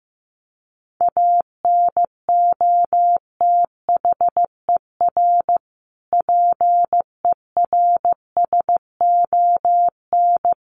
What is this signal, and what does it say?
Morse code, heard as a keyed tone, about 15 wpm, 700 Hz. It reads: ANOTHER PERSON
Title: Sentences from Top 1400 Words 15wpm